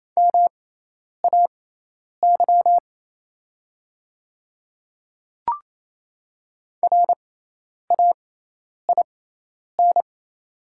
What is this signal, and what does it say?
Morse code, heard as a keyed tone, about 28 wpm, spaced as for 10 wpm, 700 Hz. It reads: MAY RAIN
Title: 6x Character Spacing - Top 300 Words - 28wpm